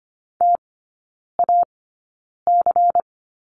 Morse code, heard as a keyed tone, about 25 wpm, spaced as for 9 wpm, 700 Hz. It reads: TAC
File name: course / 25:10wpm / 093 Lesson 16e - Letter c - Three Characters 25wpm